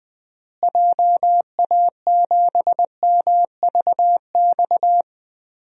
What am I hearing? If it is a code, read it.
Morse code, heard as a keyed tone, about 20 wpm, 700 Hz. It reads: JA7MVX
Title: DXCC-100 Call Signs - 6 Characters - Repeated 3x - 20wpm